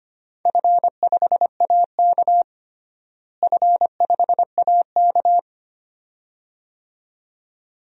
Morse code, heard as a keyed tone, about 25 wpm, 700 Hz. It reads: F5AK F5AK
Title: DXCC-100 Call Signs - 4 Characters - Repeated 2x - 25wpm